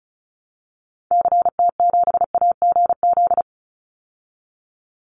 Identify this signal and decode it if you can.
Morse code, heard as a keyed tone, about 34 wpm, 700 Hz. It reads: CT7AGZ
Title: Call Signs Repeated 2x - 35wpm